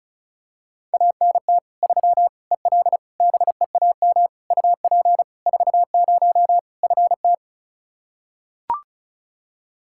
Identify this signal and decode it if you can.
Morse code, heard as a keyed tone, about 34 wpm, 700 Hz. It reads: ANT 3 EL BEAM UP 40 FT
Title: Practice QSO - Exchange 2 - Info - 35wpm